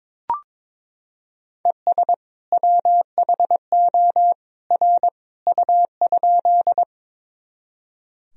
Morse code, heard as a keyed tone, about 22 wpm, 700 Hz. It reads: ES WHO R U?
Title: Sentences from Top 100 Words and Abbreviations 22wpm